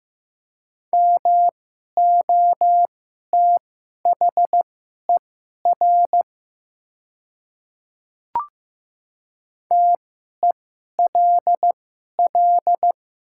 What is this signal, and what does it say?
Morse code, heard as a keyed tone, about 15 wpm, spaced as for 11 wpm, 700 Hz. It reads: MOTHER TELL
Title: 2x Character Spacing - Top 300 Words - 15wpm